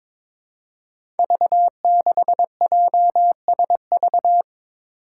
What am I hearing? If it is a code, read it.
Morse code, heard as a keyed tone, about 22 wpm, 700 Hz. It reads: V6JSV
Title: Five Challenging Letter-Number Combinations 22wpm